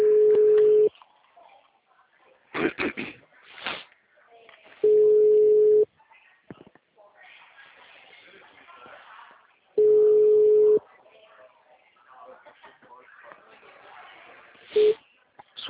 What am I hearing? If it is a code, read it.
Ringback tone (Europe, 425 Hz), with speech.
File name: out- -2005- - - .26